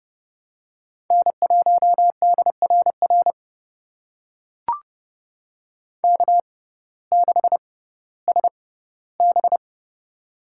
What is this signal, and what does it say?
Morse code, heard as a keyed tone, about 30 wpm, spaced as for 17 wpm, 700 Hz. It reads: N1DRR K6SB
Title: USA Call Signs - 6x Character Spacing - 30wpm